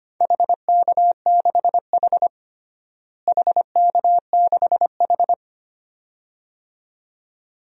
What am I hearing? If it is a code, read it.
Morse code, heard as a keyed tone, about 25 wpm, 700 Hz. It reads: HK6H HK6H
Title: DXCC-100 Call Signs - 4 Characters - Repeated 2x - 25wpm